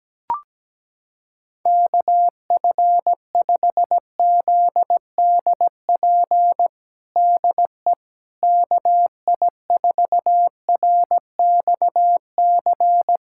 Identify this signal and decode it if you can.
Morse code, heard as a keyed tone, about 17 wpm, 700 Hz. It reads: KF5ZDP DE KI4RXC
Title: USA Callsign DE Callsign 17wpm